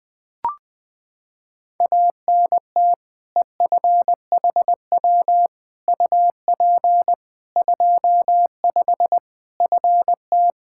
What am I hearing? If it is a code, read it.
Morse code, heard as a keyed tone, about 20 wpm, 700 Hz. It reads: ANT EFHW UP 25 FT
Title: K1USN SST QSO Practice 20wpm